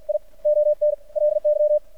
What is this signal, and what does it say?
Morse code, heard as a keyed tone, about 52 wpm, spaced as for 39 wpm, 600 Hz, with radio noise. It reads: I ONLY
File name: archived / CRSnet